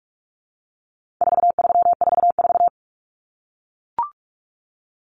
Morse code, heard as a keyed tone, about 44 wpm, 700 Hz. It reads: 4344
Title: Common Cut Numbers - 4-Digit 45wpm